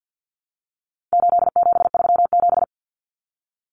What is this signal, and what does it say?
Morse code, heard as a keyed tone, about 50 wpm, 700 Hz. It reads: 8737
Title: Common Cut Numbers - 4-Digit 50wpm